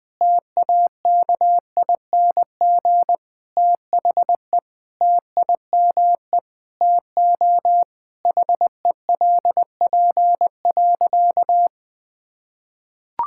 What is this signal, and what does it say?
Morse code, heard as a keyed tone, about 21 wpm, 700 Hz. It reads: TAKING THE TIME TO HELP.
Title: Sentences with 9-Word Phrases - Spoken Beginning - 20wpm